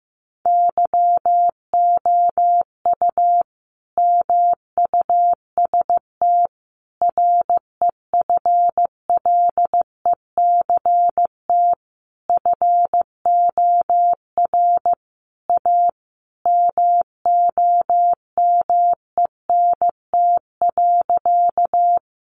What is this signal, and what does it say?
Morse code, heard as a keyed tone, about 15 wpm, 700 Hz. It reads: YOU MUST REFLECT FOR A MOMENT.
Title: Sentences from Top 1500 Words 15wpm